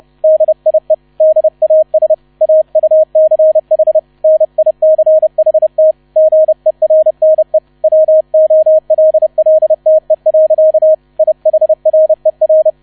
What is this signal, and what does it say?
Morse code, heard as a keyed tone, about 30 wpm, 630 Hz, with radio noise. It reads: DIE DAS AUCH NICHT GERNE WOLLTE. IHRER